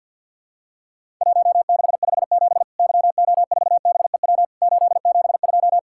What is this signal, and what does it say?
Morse code, heard as a keyed tone, about 50 wpm, 700 Hz. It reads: 1657 YO4BEW 872
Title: Worked All Europe DX Contest - QTC Exchange with Repeats - 50wpm